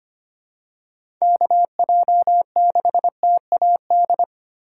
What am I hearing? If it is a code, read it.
Morse code, heard as a keyed tone, about 25 wpm, 700 Hz. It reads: KJ6TAD
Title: USA Call Signs - 6 Characters - 3x Character Spacing - 25wpm